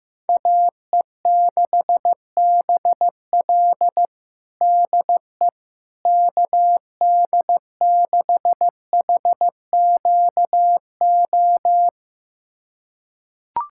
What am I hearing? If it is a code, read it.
Morse code, heard as a keyed tone, about 16 wpm, 700 Hz. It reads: AE6BL DE KD6HQO
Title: USA Callsign DE Callsign 15wpm